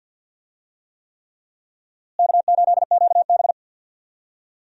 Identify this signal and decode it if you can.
Morse code, heard as a keyed tone, about 53 wpm, 700 Hz. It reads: K8QB
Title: USA Call Signs - 4 Characters - Repeated 3x - 50wpm